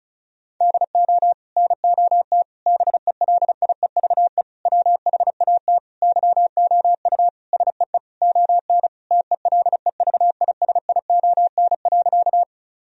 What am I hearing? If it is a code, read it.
Morse code, heard as a keyed tone, about 34 wpm, 700 Hz. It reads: DO NOT BELIEVE WHAT YOU SEE ON TELEVISION.
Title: Sentences from Top 1000 Words - Review Part 4 - 35wpm